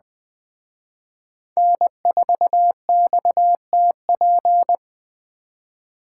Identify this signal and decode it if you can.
Morse code, heard as a keyed tone, about 21 wpm, 700 Hz. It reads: N4XTP